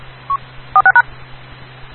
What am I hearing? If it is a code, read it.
DTMF, heard as telephone touch tones, tones of about 90 ms. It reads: 46#
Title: Rocky Mountain Radio League Repeaters